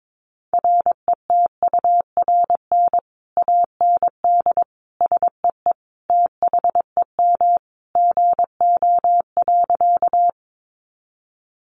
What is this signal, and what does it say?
Morse code, heard as a keyed tone, about 22 wpm, 700 Hz. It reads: RETURN AND SEE THEM GO.
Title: Sentences from Top 800 Words 22wpm